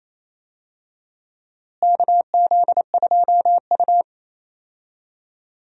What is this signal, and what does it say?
Morse code, heard as a keyed tone, about 28 wpm, 700 Hz. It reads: KZ2U
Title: USA Call Signs - 4 Characters - Repeated 2x - 28wpm